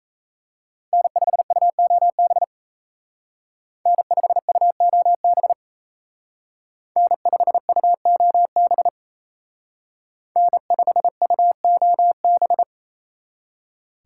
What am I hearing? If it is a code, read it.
Morse code, heard as a keyed tone, about 36 wpm, 700 Hz. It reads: N5UOB N5UOB N5UOB N5UOB